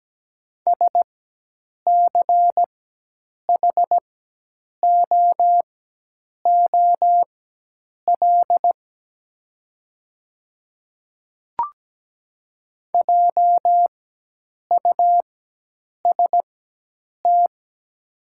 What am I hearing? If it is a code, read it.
Morse code, heard as a keyed tone, about 17 wpm, spaced as for 8 wpm, 690 Hz. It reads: SCHOOL JUST
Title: 4x Character Spacing - Top 300 Words - 17wpm